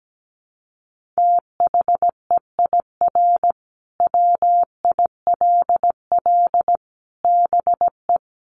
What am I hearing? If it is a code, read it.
Morse code, heard as a keyed tone, about 17 wpm, 700 Hz. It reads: THEIR WILL BE
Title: Sentences from Top 1500 Words 17wpm